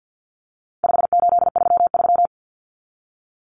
Morse code, heard as a keyed tone, about 50 wpm, 700 Hz. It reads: 5833
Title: Numbers - 4-Digit 50wpm